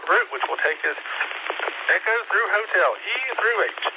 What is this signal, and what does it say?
Emergency Alert System (SAME) tones, heard as harsh buzzing bursts, with speech. It is heard from 0.5 to 3.5 s.
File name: SimplexLogic